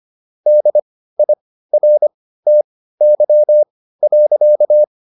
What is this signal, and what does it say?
Morse code, heard as a keyed tone, about 25 wpm, spaced as for 15 wpm, 600 Hz. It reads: DIRTY.